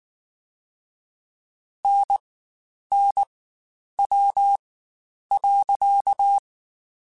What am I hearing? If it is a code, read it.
Morse code, heard as a keyed tone, about 19 wpm, 800 Hz. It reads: N N W .
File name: Spaced Characters / Spaced Characters-mp3s / Weather5 22 sp @18